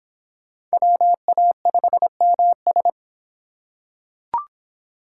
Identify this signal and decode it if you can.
Morse code, heard as a keyed tone, about 25 wpm, 700 Hz. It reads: WA5MS